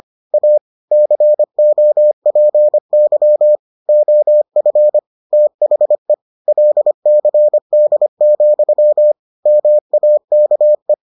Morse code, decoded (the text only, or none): A COPY OF THE LCD, MAKE